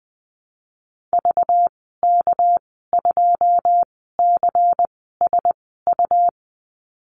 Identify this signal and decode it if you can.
Morse code, heard as a keyed tone, about 19 wpm, spaced as for 14 wpm, 700 Hz. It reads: VK2CSU
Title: 219 Lesson 35g - Number 0 - Call Signs 20wpm